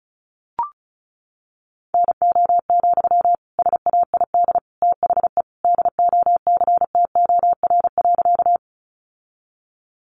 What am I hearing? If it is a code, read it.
Morse code, heard as a keyed tone, about 34 wpm, 700 Hz. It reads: NO, SAID THE DOCTOR.
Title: Sentences from Top 1500 Words 35wpm